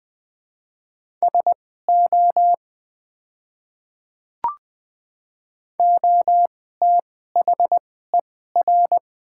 Morse code, heard as a keyed tone, about 20 wpm, spaced as for 15 wpm, 700 Hz. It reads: SO OTHER
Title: 2x Character Spacing - Top 100 Words - 20wpm